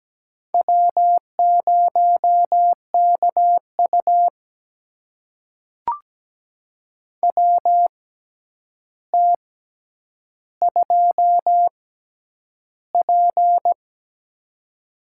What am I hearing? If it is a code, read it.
Morse code, heard as a keyed tone, about 17 wpm, spaced as for 9 wpm, 700 Hz. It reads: W0KU WT2P
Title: USA Call Signs - 4 Characters - 6x Character Spacing - 17wpm